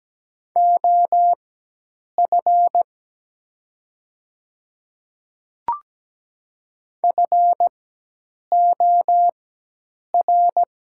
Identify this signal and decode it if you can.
Morse code, heard as a keyed tone, about 17 wpm, spaced as for 8 wpm, 700 Hz. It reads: OF FOR
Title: 4x Character Spacing - Top 100 Words - 17wpm